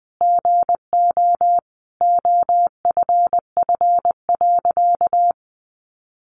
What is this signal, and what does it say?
Morse code, heard as a keyed tone, about 20 wpm, 700 Hz. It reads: GO OFF.
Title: A Sound Of Thunder 20wpm